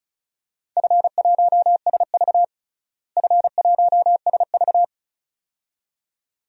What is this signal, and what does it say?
Morse code, heard as a keyed tone, about 34 wpm, 700 Hz. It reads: F1SV F1SV
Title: DXCC-100 Call Signs - 4 Characters - Repeated 2x - 35wpm